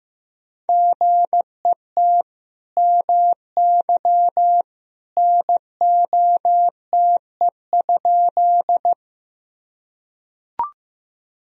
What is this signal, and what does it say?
Morse code, heard as a keyed tone, about 15 wpm, 700 Hz. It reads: GET MY NOTE?
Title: Sentences from Top 1000 Words - Review Part 2 - 15wpm